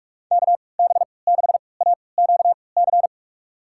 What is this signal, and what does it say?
Morse code, heard as a keyed tone, about 45 wpm, spaced as for 25 wpm, 700 Hz. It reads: KB6AQC
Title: USA Call Signs - 3x Character Spacing - 45wpm